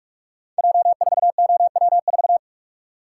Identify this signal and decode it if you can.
Morse code, heard as a keyed tone, about 44 wpm, 700 Hz. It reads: J4OW4
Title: Code Groups - 5 Letter:Numbers - 45wpm